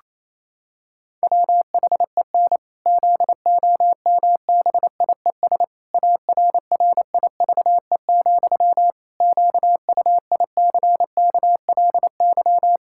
Morse code, heard as a keyed tone, about 27 wpm, 700 Hz. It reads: WHEN ZOMBIES ARRIVE, QUICKLY
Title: Pangram Challenge v2 - 28wpm